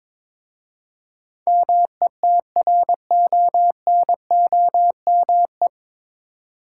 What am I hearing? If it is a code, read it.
Morse code, heard as a keyed tone, about 22 wpm, 700 Hz. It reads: METRONOME
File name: practice / 22wpm / Difficult Words with Hints - 22wpm